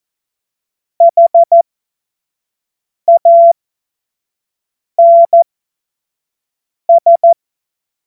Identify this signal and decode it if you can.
Morse code, heard as a keyed tone, about 14 wpm, spaced as for 5 wpm, 680 Hz. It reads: HANS
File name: Morsekurs / Witze / Witz 15. Juni 2021, 25 F70